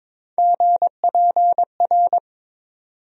Morse code, heard as a keyed tone, about 22 wpm, 700 Hz. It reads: GPR
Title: USA Call Sign Postfixes - 3 Letters 22wpm